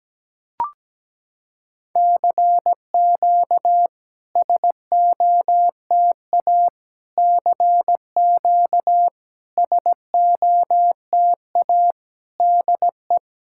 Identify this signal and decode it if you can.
Morse code, heard as a keyed tone, about 17 wpm, 700 Hz. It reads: CQ SOTA CQ SOTA DE
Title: SOTA QSO Practice 17wpm